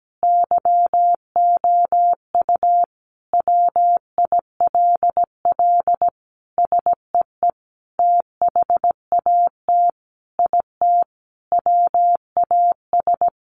Morse code, heard as a keyed tone, about 17 wpm, 700 Hz. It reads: YOU WILL SEE THAT IT WAS
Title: Sentences from Top 800 Words 17wpm